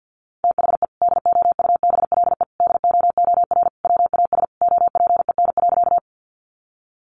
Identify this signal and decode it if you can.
Morse code, heard as a keyed tone, about 50 wpm, 700 Hz. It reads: THE DOUBLE DOOR WAS OPEN.
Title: Sentences from Top 1400 Words 50wpm